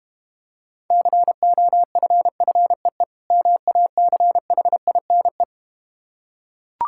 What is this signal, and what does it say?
Morse code, heard as a keyed tone, about 33 wpm, 700 Hz. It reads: COFFEE MACHINE